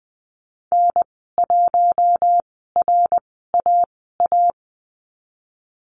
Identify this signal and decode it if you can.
Morse code, heard as a keyed tone, about 19 wpm, spaced as for 14 wpm, 700 Hz. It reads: N1RAA
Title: 088 Lesson 15g - Number 5 - Call Signs 20wpm